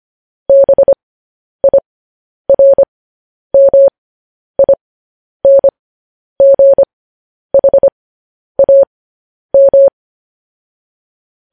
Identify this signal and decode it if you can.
Morse code, heard as a keyed tone, about 25 wpm, spaced as for 10 wpm, 550 Hz. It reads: BIRMINGHAM
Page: news headlines in morse code at 10 wpm